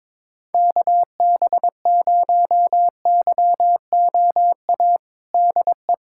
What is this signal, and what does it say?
Morse code, heard as a keyed tone, about 22 wpm, 700 Hz. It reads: KB0YOA DE
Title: USA Callsign DE Callsign - 6 Characters - 22wpm